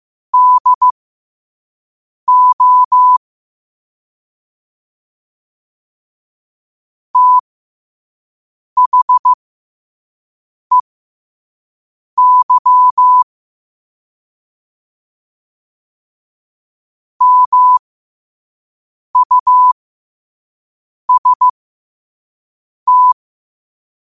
Morse code, heard as a keyed tone, about 15 wpm, spaced as for 5 wpm, 1000 Hz. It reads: DO THEY MUST
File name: qotdfiles / 5wpm